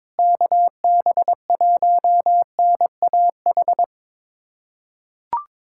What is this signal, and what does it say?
Morse code, heard as a keyed tone, about 22 wpm, 700 Hz. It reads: KB1NAH
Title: USA Callsign DE Callsign 22wpm